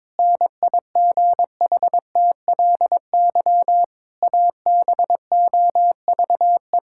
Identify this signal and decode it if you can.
Morse code, heard as a keyed tone, about 22 wpm, 700 Hz. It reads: NIGHTLY ABOVE
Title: Pangram Challenge v2 - 22wpm